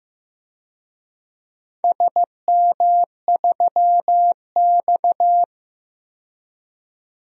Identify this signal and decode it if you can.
Morse code, heard as a keyed tone, about 15 wpm, 700 Hz. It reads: SM3X
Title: DXCC-100 Call Signs - 4 Characters - 15wpm